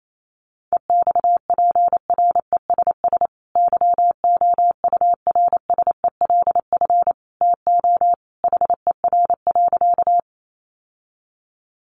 Morse code, heard as a keyed tone, about 28 wpm, 700 Hz. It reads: EXPRESS YOURSELF TO HER.
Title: Sentences from Top 1100 Words 28wpm